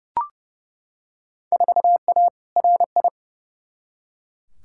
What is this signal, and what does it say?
Morse code, heard as a keyed tone, about 30 wpm, 700 Hz. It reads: 4A RI
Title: Field Day Exchanges Encore - 30wpm